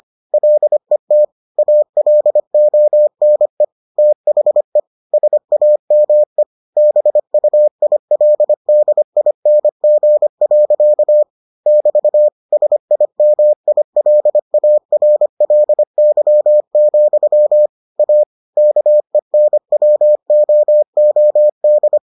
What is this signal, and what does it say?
Morse code, heard as a keyed tone, about 25 wpm, 600 Hz. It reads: LET ALONE THE SAME BUILDING. <BT> SIMILARLY, A KENWOOD